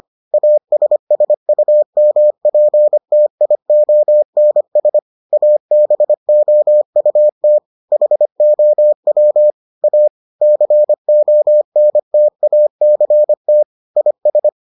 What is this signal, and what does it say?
Morse code, heard as a keyed tone, about 25 wpm, 600 Hz. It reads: ASSUMPTIONS ABOUT HOW A CONTACT IS